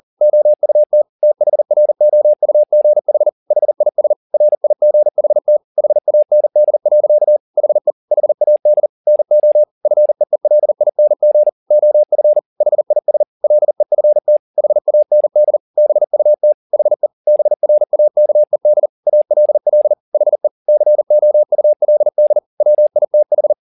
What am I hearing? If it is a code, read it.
Morse code, heard as a keyed tone, about 40 wpm, 600 Hz. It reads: OUT THROUGH HIS RIGHT HAND. HE HAD NO FEELING OF HIS LEFT HAND BUT HE BRAKED ALL HE COULD WITH
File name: the old man and the sea 2 40wpm